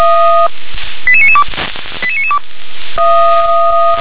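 Five-tone selective call (ZVEI), heard as quick runs of five tones, tones of about 70 ms. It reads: 88992 88992
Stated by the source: ZVEI five-tone selective call received on SDR.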